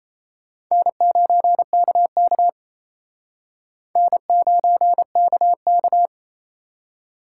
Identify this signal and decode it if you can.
Morse code, heard as a keyed tone, about 34 wpm, 700 Hz. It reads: N9KK N9KK
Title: Speed Racing - USA 1x2 Call Signs 28wpm